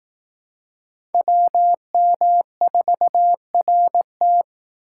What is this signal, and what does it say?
Morse code, heard as a keyed tone, about 18 wpm, 700 Hz. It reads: WM4RT
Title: Speed Racing - USA 2x2 Call Signs 15wpm